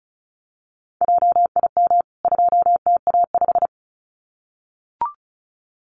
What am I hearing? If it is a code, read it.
Morse code, heard as a keyed tone, about 34 wpm, 700 Hz. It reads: JIM 2TA5
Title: CWT Exchange 35wpm